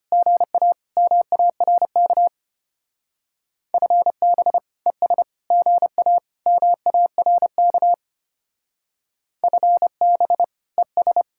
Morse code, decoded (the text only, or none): GA MARK FB ES GA MARK FB ES